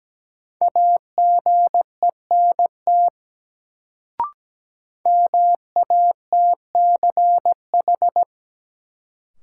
Morse code, heard as a keyed tone, about 17 wpm, 700 Hz. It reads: AGENT MATCH
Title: Sentences from Top 1600 Words - New Word Drill Alt - 17wpm